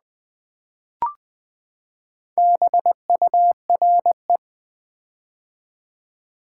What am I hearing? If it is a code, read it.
Morse code, heard as a keyed tone, about 20 wpm, 700 Hz. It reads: BURE
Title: Build A Word - 4 Letters At A Time - 11-Letter Words - 20wpm